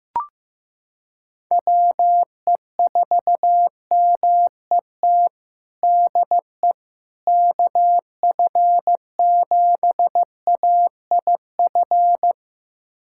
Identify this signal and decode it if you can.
Morse code, heard as a keyed tone, about 16 wpm, 700 Hz. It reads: WE4MET DE KF7AIF
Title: USA Callsign DE Callsign - 6 Characters - 15wpm